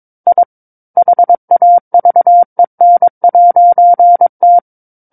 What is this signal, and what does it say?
Morse code, heard as a keyed tone, about 22 wpm, 700 Hz. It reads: I HAVEN'T
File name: 06-Chapter-06 22 WPM